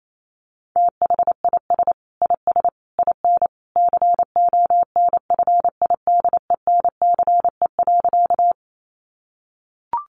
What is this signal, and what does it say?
Morse code, heard as a keyed tone, about 28 wpm, 700 Hz. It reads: THIS IS IN CONFIDENCE.
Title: Sentences from Top 1500 Words 28wpm